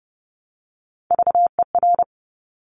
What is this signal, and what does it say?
Morse code, heard as a keyed tone, about 30 wpm, 700 Hz. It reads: VER
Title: Top Three Letter Combinations in Words 30wpm